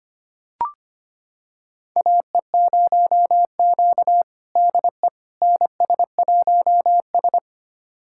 Morse code, heard as a keyed tone, about 25 wpm, 700 Hz. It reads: AE0Q DE NS1S
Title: USA Callsign DE Callsign - 4 Characters - 25wpm